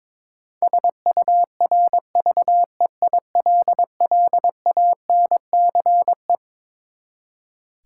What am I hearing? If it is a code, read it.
Morse code, heard as a keyed tone, about 22 wpm, 700 Hz. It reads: SURVEILLANCE
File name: Twelve Letter Words - 22wpm